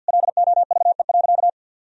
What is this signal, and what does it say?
Morse code, heard as a keyed tone, about 52 wpm, 700 Hz. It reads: LOVE.
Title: Sentences from Top 1000 Words - Review Part 3 - 50wpm